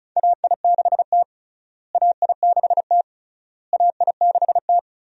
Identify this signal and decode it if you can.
Morse code, heard as a keyed tone, about 36 wpm, 700 Hz. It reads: AI6T AI6T AI6T
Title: USA Call Signs - 4 Characters - Repeated 3x - 35wpm